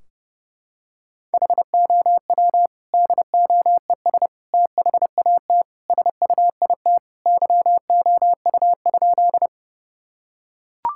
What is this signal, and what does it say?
Morse code, heard as a keyed tone, about 32 wpm, 700 Hz. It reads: HOW DOES THAT SUIT YOU?
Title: Sentences from Top 1600 Words 30wpm